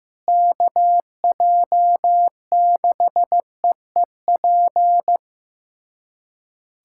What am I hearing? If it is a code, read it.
Morse code, heard as a keyed tone, about 15 wpm, 700 Hz. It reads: KJ6EEP
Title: USA Callsign DE Callsign 15wpm